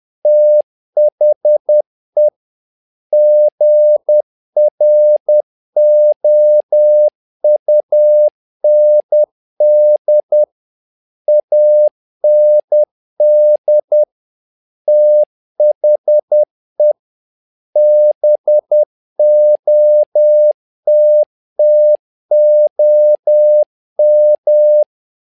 Morse code, heard as a keyed tone, about 10 wpm, 600 Hz. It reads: THE GROUND AND THE BOTTOM
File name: learning center / tower trouble / tower trouble 10wpm